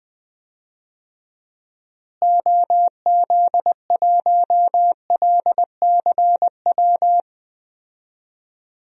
Morse code, heard as a keyed tone, about 20 wpm, 700 Hz. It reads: OZ1LCW
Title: DXCC-100 Call Signs - 6 Characters - Repeated 2x - 20wpm